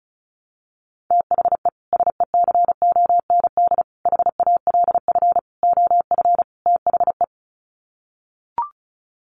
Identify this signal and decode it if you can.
Morse code, heard as a keyed tone, about 34 wpm, 700 Hz. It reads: THE SECOND HALF OF THE
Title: Sets of 5 Words 35wpm